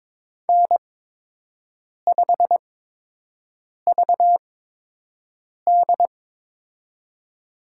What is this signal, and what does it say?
Morse code, heard as a keyed tone, about 22 wpm, spaced as for 6 wpm, 700 Hz. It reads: N5VD
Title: USA Call Signs - 4 Characters - 8x Character Spacing - 22wpm